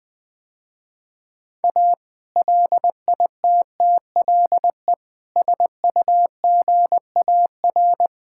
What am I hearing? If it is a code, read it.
Morse code, heard as a keyed tone, about 21 wpm, 700 Hz. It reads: A LITTLE SUGAR
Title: Sentences with 2-Word Phrases - Spoken Beginning - 20wpm